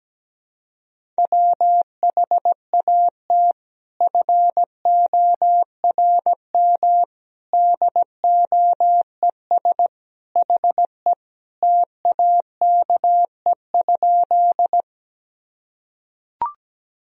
Morse code, heard as a keyed tone, about 17 wpm, 700 Hz. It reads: WHAT FORM DOES HE TAKE?
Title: Sentences from Top 1000 Words - Review Part 1 - 17wpm